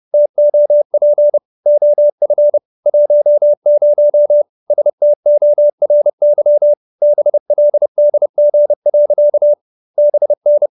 Morse code, heard as a keyed tone, about 30 wpm, 590 Hz. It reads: TOP OF 10 STORY BLDG. BN
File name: QSO315 30